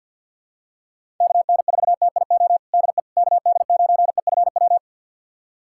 Morse code, heard as a keyed tone, about 51 wpm, 700 Hz. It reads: KN4TIO DE KD9EFW
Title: USA Callsign DE Callsign - 6 Characters - 50wpm